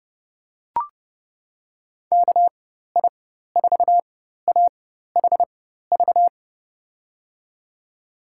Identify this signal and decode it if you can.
Morse code, heard as a keyed tone, about 30 wpm, spaced as for 14 wpm, 700 Hz. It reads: KI4AHV
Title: USA Call Signs - 6 Characters - 4x Character Spacing - 30wpm